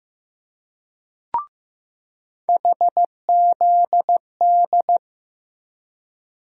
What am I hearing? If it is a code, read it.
Morse code, heard as a keyed tone, about 15 wpm, 700 Hz. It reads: HZD
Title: USA Call Sign Postfixes - 3 Letters 15wpm